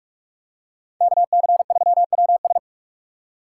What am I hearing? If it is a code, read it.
Morse code, heard as a keyed tone, about 42 wpm, 700 Hz. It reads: KC3WS